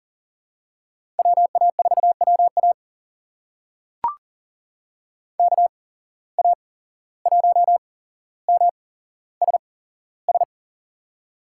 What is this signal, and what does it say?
Morse code, heard as a keyed tone, about 40 wpm, spaced as for 16 wpm, 690 Hz. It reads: WA4WA KA1MSS
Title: USA Call Signs - 8x Character Spacing - 40wpm